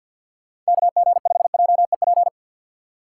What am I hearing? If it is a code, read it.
Morse code, heard as a keyed tone, about 59 wpm, 700 Hz. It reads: KG5JEP
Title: Speed Racing - USA Call Signs 50wpm